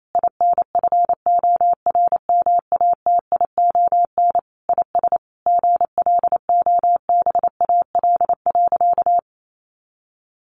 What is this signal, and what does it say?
Morse code, heard as a keyed tone, about 28 wpm, 700 Hz. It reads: INFORMATION IS GLOBAL.
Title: Sentences from Top 1200 Words 28wpm